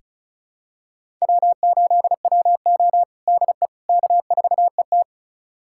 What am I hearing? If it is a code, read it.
Morse code, heard as a keyed tone, about 34 wpm, 700 Hz. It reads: W8WO DE K4ET
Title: USA Callsign DE Callsign - 4 Characters - 35wpm